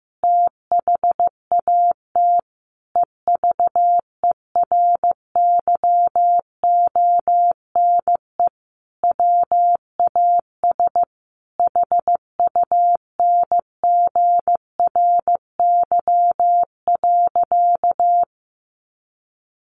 Morse code, heard as a keyed tone, about 15 wpm, 700 Hz. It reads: THAT EVERYONE WAS HUNGRY.
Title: Sentences from Top 1400 Words 15wpm